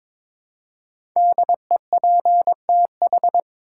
Morse code, heard as a keyed tone, about 22 wpm, 700 Hz. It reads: DEPTH
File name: 5 Letter Words - First 3 Letters Given - 22wpm